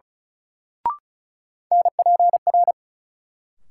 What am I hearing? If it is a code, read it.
Morse code, heard as a keyed tone, about 36 wpm, 700 Hz. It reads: NPR
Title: Sets of 5 Words - New Word Drill - 35wpm